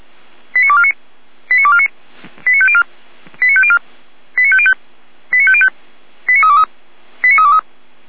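Five-tone selective call (ZVEI), heard as quick runs of five tones, tones of about 70 ms. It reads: 89249 89249 89504 89504 89505 89505 89323 89323
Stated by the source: ZVEI five-tone selective call received on SDR.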